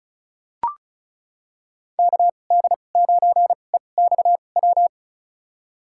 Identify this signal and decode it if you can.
Morse code, heard as a keyed tone, about 34 wpm, spaced as for 25 wpm, 700 Hz. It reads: KD9EXW
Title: USA Call Signs - 2x Character Spacing - 35wpm